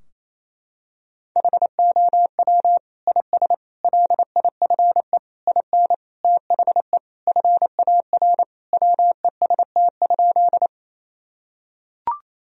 Morse code, heard as a keyed tone, about 28 wpm, 700 Hz. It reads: HOW IS LIFE IN THE FAR WEST?
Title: Sentences from Top 1000 Words - Review Part 2 - 28wpm